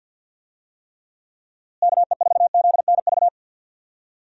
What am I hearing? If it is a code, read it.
Morse code, heard as a keyed tone, about 50 wpm, 700 Hz. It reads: KE4ZNV
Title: USA Call Signs - 6 Characters - Repeated 3x - 50wpm